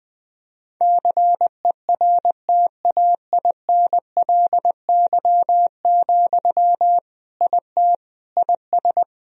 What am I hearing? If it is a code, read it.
Morse code, heard as a keyed tone, about 21 wpm, 700 Hz. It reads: CERTAINLY, IT IS
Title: Sentences from Top 1000 Words - Review Part 2 - 20wpm